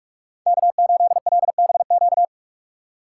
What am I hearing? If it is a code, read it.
Morse code, heard as a keyed tone, about 44 wpm, 700 Hz. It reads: K8LBQ